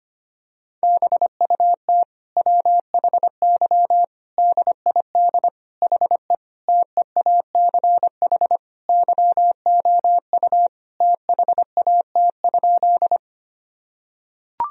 Morse code, heard as a keyed tone, about 25 wpm, 700 Hz. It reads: BUT WHY DID HE TEACH YOU THAT?
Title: Sentences from Top 1000 Words - Review Part 2 - 25wpm